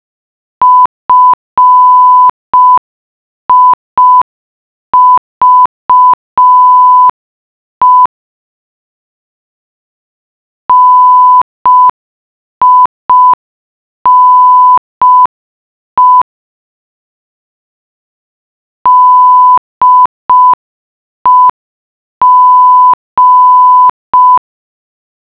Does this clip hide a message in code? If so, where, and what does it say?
Yes: Morse code from 0.6 to 24.4 s, heard as a keyed tone, about 5 wpm, 1000 Hz. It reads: FIVE NINE DEG